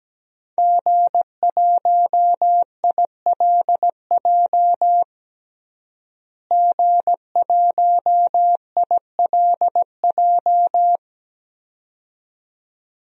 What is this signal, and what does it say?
Morse code, heard as a keyed tone, about 17 wpm, 700 Hz. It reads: G1ILJ G1ILJ
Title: DXCC-100 Call Signs - 5 Characters - Repeated 2x - 17wpm